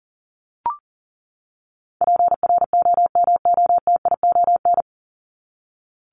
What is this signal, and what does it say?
Morse code, heard as a keyed tone, about 40 wpm, 700 Hz. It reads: PROMOTION
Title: Top 1501-1750 Words 40wpm